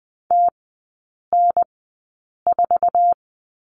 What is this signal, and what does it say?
Morse code, heard as a keyed tone, about 20 wpm, spaced as for 8 wpm, 700 Hz. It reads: TN4
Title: 047 Lesson 9e - Number 4 - Three Characters 20wpm